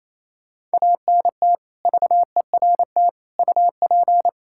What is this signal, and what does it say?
Morse code, heard as a keyed tone, about 27 wpm, 700 Hz. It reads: ANT VERT UP
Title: Practice QSO - Exchange 2 - Info - 28wpm